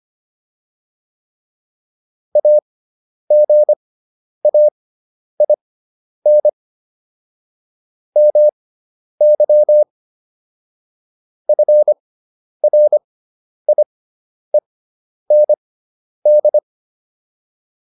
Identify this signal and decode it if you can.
Morse code, heard as a keyed tone, about 25 wpm, spaced as for 10 wpm, 600 Hz. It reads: AGAIN MY FRIEND